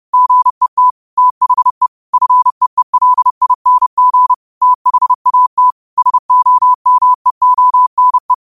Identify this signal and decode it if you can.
Morse code, heard as a keyed tone, about 30 wpm, 1000 Hz. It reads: GET THE FEELING THAT SOMEONE
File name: qotdfiles / 30wpm